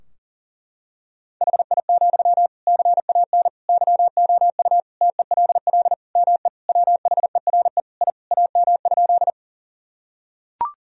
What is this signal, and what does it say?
Morse code, heard as a keyed tone, about 40 wpm, 700 Hz. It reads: HI, CAN YOU TELL ME WHERE I AM?
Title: Sentences from Top 1000 Words - Review Part 3 - 40wpm